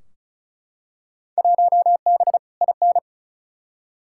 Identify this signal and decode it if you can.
Morse code, heard as a keyed tone, about 36 wpm, 700 Hz. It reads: 1B IN
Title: Field Day Exchanges - 35wpm